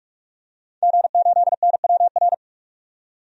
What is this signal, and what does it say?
Morse code, heard as a keyed tone, about 46 wpm, 700 Hz. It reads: G8NWR
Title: DXCC-100 Call Signs - 5 Characters - 45wpm